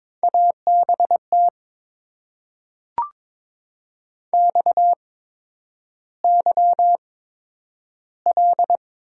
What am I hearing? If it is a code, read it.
Morse code, heard as a keyed tone, about 22 wpm, spaced as for 10 wpm, 690 Hz. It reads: ABT XYL
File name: Top 90 QSO Elements - 8x Character Spacing - 22wpm